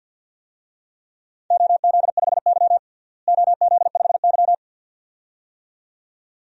Morse code, heard as a keyed tone, about 47 wpm, 700 Hz. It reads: OZ5Y OZ5Y
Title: DXCC-100 Call Signs - 4 Characters - Repeated 2x - 50wpm